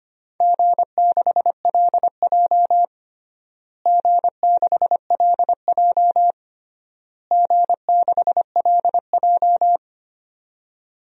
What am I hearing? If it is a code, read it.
Morse code, heard as a keyed tone, about 25 wpm, 700 Hz. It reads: G6LJ G6LJ G6LJ
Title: DXCC-100 Call Signs - 4 Characters - Repeated 3x - 25wpm